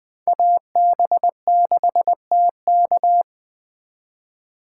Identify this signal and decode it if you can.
Morse code, heard as a keyed tone, about 21 wpm, 700 Hz. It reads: AB6TK